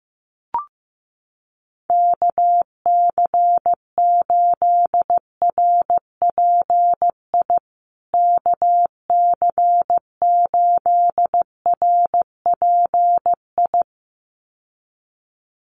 Morse code, heard as a keyed tone, about 15 wpm, 700 Hz. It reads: KC8RPI KC8RPI
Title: USA Call Signs Repeated 2x 15wpm